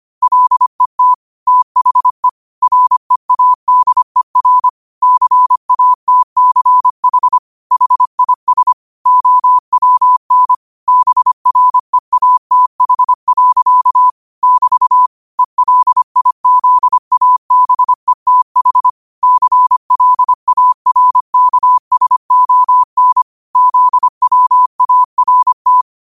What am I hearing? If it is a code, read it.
Morse code, heard as a keyed tone, about 25 wpm, 1000 Hz. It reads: LET THE READER CATCH HIS OWN BREATH. <BT> ELIZABETH CLARKSON ZWART
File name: qotdfiles / 25wpm